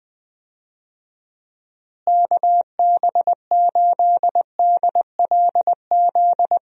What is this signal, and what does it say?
Morse code, heard as a keyed tone, about 21 wpm, 700 Hz. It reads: KB8DLZ
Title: USA Call Signs - 6 Characters - Repeated 3x - 20wpm